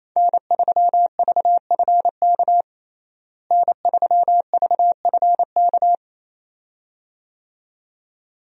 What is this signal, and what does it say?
Morse code, heard as a keyed tone, about 27 wpm, 700 Hz. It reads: N3VFK N3VFK
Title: USA Call Signs - 5 Characters - Repeated 2x - 28wpm